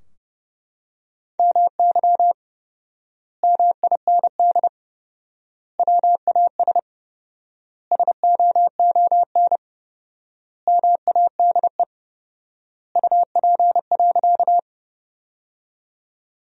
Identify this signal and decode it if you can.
Morse code, heard as a keyed tone, about 30 wpm, 700 Hz. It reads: MY MIND WAS SOON MADE UP.
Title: 4x Word-Spacing - Sentences from Top 500 Words - 30wpm